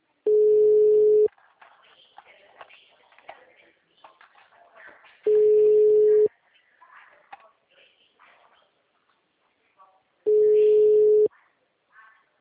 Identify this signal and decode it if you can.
Ringback tone (Europe, 425 Hz).